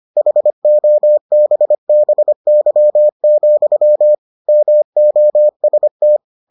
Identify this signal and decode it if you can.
Morse code, heard as a keyed tone, about 25 wpm, 600 Hz. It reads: HOBBY, MOST